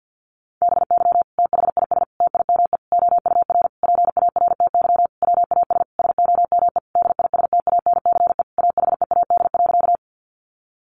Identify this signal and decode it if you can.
Morse code, heard as a keyed tone, about 50 wpm, 700 Hz. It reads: BY THIS TIME OUR PARTY WAS SOME DISTANCE AHEAD.
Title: Sentences from Top 1400 Words 50wpm